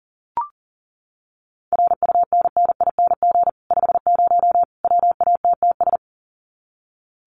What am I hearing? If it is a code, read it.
Morse code, heard as a keyed tone, about 40 wpm, 700 Hz. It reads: RUNNING 50 WATTS
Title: Common QSO Phrases 40wpm